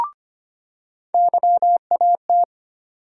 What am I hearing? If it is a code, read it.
Morse code, heard as a keyed tone, about 25 wpm, 700 Hz. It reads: YAT
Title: USA Call Sign Postfixes - 3 Letters 25wpm